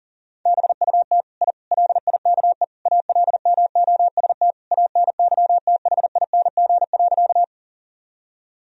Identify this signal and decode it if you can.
Morse code, heard as a keyed tone, about 40 wpm, 700 Hz. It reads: BUT I LIKE ALMOST ANYTHING.
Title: Sentences from Top 1000 Words - Review Part 4 - 40wpm